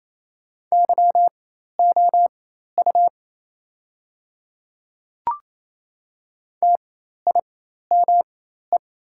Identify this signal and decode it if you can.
Morse code, heard as a keyed tone, about 28 wpm, spaced as for 13 wpm, 700 Hz. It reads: YOU TIME
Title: 4x Character Spacing - Top 300 Words - 28wpm